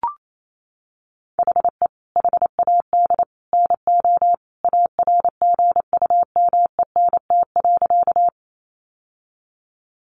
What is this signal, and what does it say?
Morse code, heard as a keyed tone, about 28 wpm, 700 Hz. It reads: HE HAD NO ARGUMENT.
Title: Sentences from Top 1400 Words 28wpm